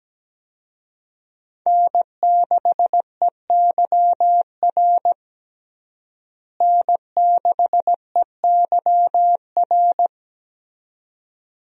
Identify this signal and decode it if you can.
Morse code, heard as a keyed tone, about 17 wpm, 700 Hz. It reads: N6EYR N6EYR
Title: USA Call Signs - 5 Characters - Repeated 2x - 17wpm